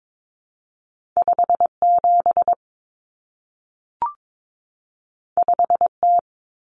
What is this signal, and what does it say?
Morse code, heard as a keyed tone, about 22 wpm, 700 Hz. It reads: 57 5T
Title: Common Cut Numbers - 2-Digit 22wpm